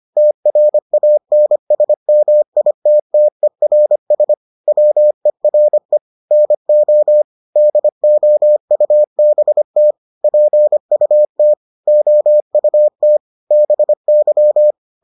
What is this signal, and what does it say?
Morse code, heard as a keyed tone, about 25 wpm, 600 Hz. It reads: TRANSMITTERS WERE NO DOUBT PUT OUT BY